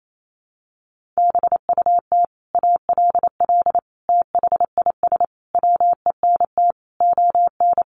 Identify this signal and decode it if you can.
Morse code, heard as a keyed tone, about 28 wpm, 700 Hz. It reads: BUT ALL THIS WENT ON
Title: Sentences from Top 1500 Words 28wpm